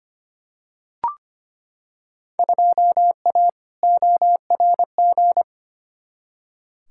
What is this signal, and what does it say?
Morse code, heard as a keyed tone, about 25 wpm, 700 Hz. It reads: 2A ORG
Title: Field Day Exchanges Encore - 25wpm